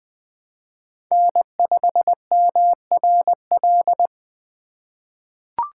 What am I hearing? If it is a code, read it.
Morse code, heard as a keyed tone, about 20 wpm, 700 Hz. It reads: N5MRL
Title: USA Call Signs - 5 Characters - 2x Character Spacing - 20wpm